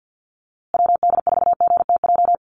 Morse code, heard as a keyed tone, about 50 wpm, 700 Hz. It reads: WD4GTJ
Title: USA Call Signs Repeated 2x 50wpm